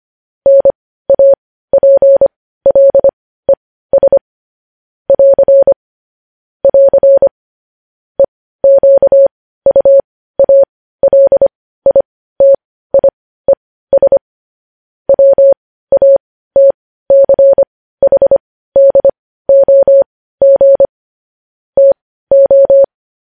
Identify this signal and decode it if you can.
Morse code, heard as a keyed tone, about 25 wpm, spaced as for 15 wpm, 550 Hz. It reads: NAPLES <AR> <AR> EQUALITIES WATCHDOG TO
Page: News Headlines in Morse Code at 15 WPM